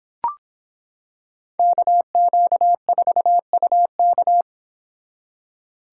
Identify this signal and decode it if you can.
Morse code, heard as a keyed tone, about 26 wpm, 700 Hz. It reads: KQ4UK